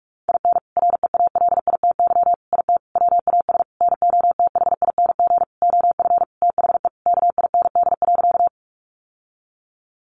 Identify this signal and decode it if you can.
Morse code, heard as a keyed tone, about 44 wpm, 700 Hz. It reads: IN REALITY IT WAS NOTHING OF THE KIND.
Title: Sentences from Top 1200 Words 45wpm